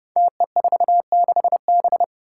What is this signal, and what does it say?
Morse code, heard as a keyed tone, about 30 wpm, 700 Hz. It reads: TE46B
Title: Five Challenging Letter-Number Combinations 30wpm